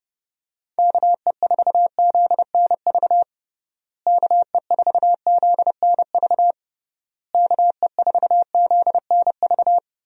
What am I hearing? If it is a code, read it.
Morse code, heard as a keyed tone, about 32 wpm, 700 Hz. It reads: KE4ZNV KE4ZNV KE4ZNV
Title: USA Call Signs - 6 Characters - Repeated 3x - 30wpm